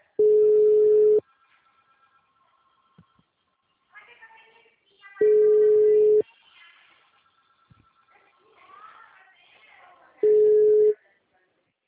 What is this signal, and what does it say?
Ringback tone (Europe, 425 Hz).
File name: out- -2021- - - .68590